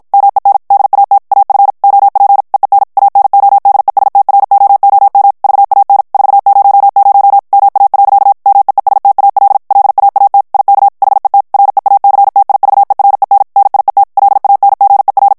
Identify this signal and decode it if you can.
Morse code, heard as a keyed tone, about 51 wpm, 800 Hz. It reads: MEN DAT AF OP EEN ANODESTROOM VAN 400 MA. MEESTAL LAAT IK HET REACTIVEREN NIET LANGER
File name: HOLLANDAIS texte clair 2 vitesse 50 mots minute